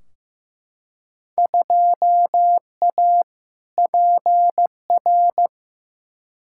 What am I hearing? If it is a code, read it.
Morse code, heard as a keyed tone, about 15 wpm, 700 Hz. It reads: 2A PR
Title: Field Day Exchanges Encore - 15wpm